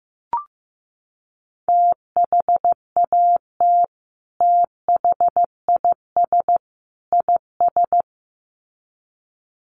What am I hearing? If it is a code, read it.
Morse code, heard as a keyed tone, about 15 wpm, 700 Hz. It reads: THAT THIS IS
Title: Sets of 3 Words 15wpm